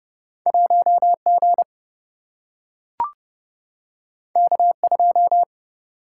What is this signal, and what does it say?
Morse code, heard as a keyed tone, about 30 wpm, 700 Hz. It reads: 1 G K 2
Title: Single-Letter Single-Number 30wpm